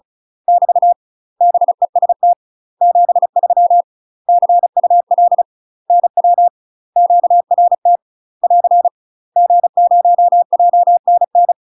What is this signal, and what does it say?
Morse code, heard as a keyed tone, about 36 wpm, 700 Hz. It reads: <BT> BEST 73 CUL NW QRT <AR> G0JNN
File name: UK MASTER QSOs TRAINING / 35WPM / UK MASTER QSOs 0029